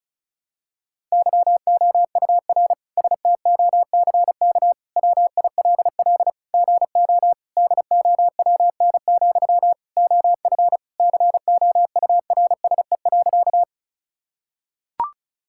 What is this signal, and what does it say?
Morse code, heard as a keyed tone, about 36 wpm, 700 Hz. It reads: YOUR STOCK WILL GO DOWN, OF COURSE.
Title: Sentences from Top 1000 Words - Review Part 5 - 35wpm